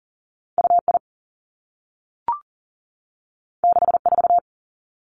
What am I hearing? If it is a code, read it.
Morse code, heard as a keyed tone, about 39 wpm, 700 Hz. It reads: UI 64
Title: Two Challenging Letter-Number Combinations 40wpm